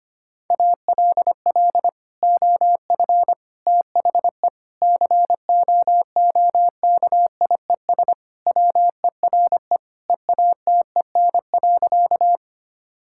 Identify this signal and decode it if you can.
Morse code, heard as a keyed tone, about 25 wpm, 700 Hz. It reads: ALL OF THE COOKIES WERE EATEN.
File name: practice / 25wpm / Sentences with 2-Word Phrases - 25wpm